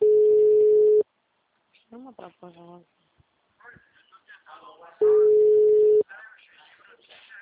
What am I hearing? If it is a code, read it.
Ringback tone (Europe, 425 Hz), with speech.